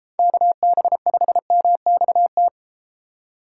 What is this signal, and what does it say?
Morse code, heard as a keyed tone, about 32 wpm, 700 Hz. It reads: KB5MXT